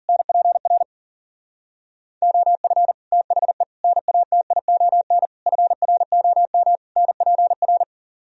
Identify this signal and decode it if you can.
Morse code, heard as a keyed tone, about 51 wpm, 700 Hz. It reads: NPR OF THE NATION FROM NPR